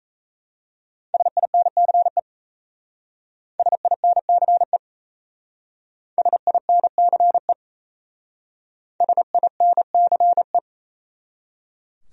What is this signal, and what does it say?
Morse code, heard as a keyed tone, about 34 wpm, 700 Hz. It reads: SINCE SINCE SINCE SINCE
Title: Speed-Racing Top 300 Words 28wpm